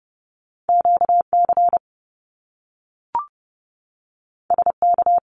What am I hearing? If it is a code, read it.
Morse code, heard as a keyed tone, about 30 wpm, 700 Hz. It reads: QC SK
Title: Canadian Provinces and Territories 30wpm